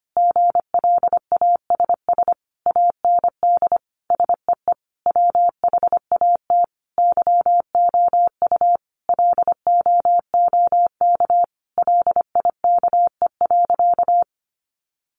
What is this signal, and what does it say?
Morse code, heard as a keyed tone, about 25 wpm, 700 Hz. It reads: GLASS AND SEE WHAT YOU LOOK LIKE.
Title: Sentences from Top 1300 Words 25wpm